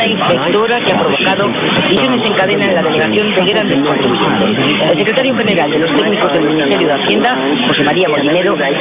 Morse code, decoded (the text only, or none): none